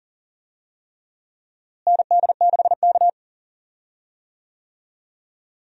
Morse code, heard as a keyed tone, about 37 wpm, 700 Hz. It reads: ND6K
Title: USA Call Signs - 4 Characters - Repeated 3x - 40wpm